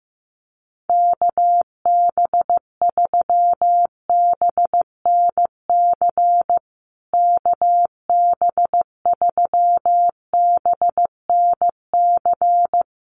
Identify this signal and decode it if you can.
Morse code, heard as a keyed tone, about 14 wpm, 700 Hz. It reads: KB3BNC KB3BNC
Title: USA Call Signs Repeated 3x 15wpm